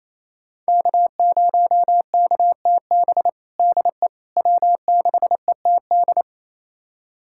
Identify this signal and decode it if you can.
Morse code, heard as a keyed tone, about 28 wpm, 700 Hz. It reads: K0KTB DE W6ETD
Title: USA Callsign DE Callsign - 5 Characters - 28wpm